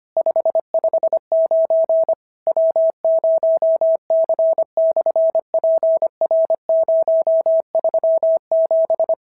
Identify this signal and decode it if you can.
Morse code, heard as a keyed tone, about 25 wpm, 650 Hz. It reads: 559 W0C/PR037
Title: SOTA QSO Practice 25wpm